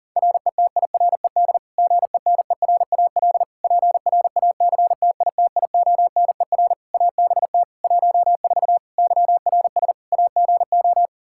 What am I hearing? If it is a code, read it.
Morse code, heard as a keyed tone, about 40 wpm, 700 Hz. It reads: RETIRED GENERAL PRACTITIONER ABT 14 YRS AGO
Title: Practice QSO - Exchange 3 - More Info - 40wpm